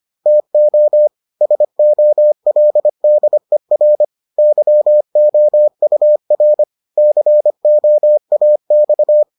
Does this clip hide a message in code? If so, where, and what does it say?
Yes: Morse code from 0.3 to 9.2 s, heard as a keyed tone, about 25 wpm, 600 Hz. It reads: TO SOLDER YOUR COAX